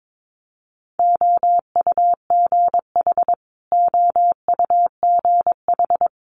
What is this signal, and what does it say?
Morse code, heard as a keyed tone, about 22 wpm, 700 Hz. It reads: OUGH OUGH
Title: Multi-letter Phonemes Repeated 3x 22wpm